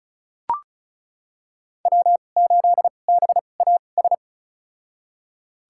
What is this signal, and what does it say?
Morse code, heard as a keyed tone, about 36 wpm, spaced as for 25 wpm, 700 Hz. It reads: W8BAS
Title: USA Call Signs - 2x Character Spacing - 35wpm